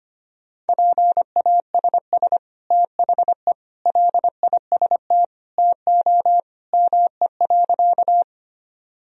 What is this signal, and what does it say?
Morse code, heard as a keyed tone, about 25 wpm, 700 Hz. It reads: PASS THE LIST TO ME.
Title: Sentences from Top 1000 Words - Review Part 2 - 25wpm